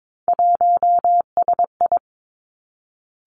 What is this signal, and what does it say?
Morse code, heard as a keyed tone, about 22 wpm, 700 Hz. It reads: 1SI
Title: Three Challenging Letter-Number Combinations 22wpm